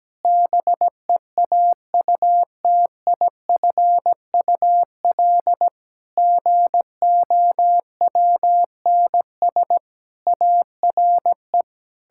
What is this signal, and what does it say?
Morse code, heard as a keyed tone, about 17 wpm, 700 Hz. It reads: BEAUTIFUL GOWNS ARE
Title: Pangram Challenge v2 - 17wpm